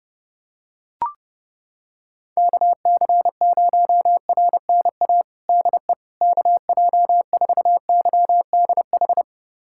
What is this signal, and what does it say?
Morse code, heard as a keyed tone, about 30 wpm, 700 Hz. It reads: KC0RNA DE KJ4YDH
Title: USA Callsign DE Callsign 30wpm